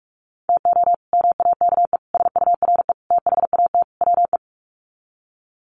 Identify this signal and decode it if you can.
Morse code, heard as a keyed tone, about 44 wpm, 700 Hz. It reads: TO MAKE SURE THAT WE
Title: Sets of 5 Words Encore 45wpm